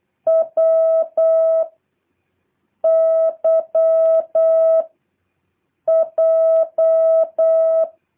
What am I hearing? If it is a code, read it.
Morse code, heard as a keyed tone, about 8 wpm, 650 Hz. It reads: W Y J
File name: jh8cbh / 2023cw